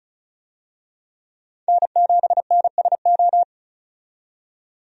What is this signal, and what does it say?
Morse code, heard as a keyed tone, about 36 wpm, 700 Hz. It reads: N7NSO
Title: USA Call Signs - 5 Characters - 35wpm